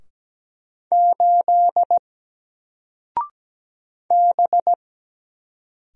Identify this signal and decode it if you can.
Morse code, heard as a keyed tone, about 17 wpm, 700 Hz. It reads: 8 B
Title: Single Letter-Number Alt 17wpm